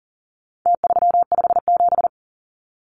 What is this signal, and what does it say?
Morse code, heard as a keyed tone, about 40 wpm, 700 Hz. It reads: T357
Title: Numbers - Contest Style - 4-Digit 40wpm